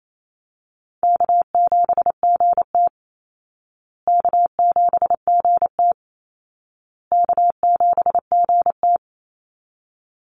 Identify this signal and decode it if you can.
Morse code, heard as a keyed tone, about 28 wpm, 700 Hz. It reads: K7GT K7GT K7GT
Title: Call Signs Repeated 2x - 28wpm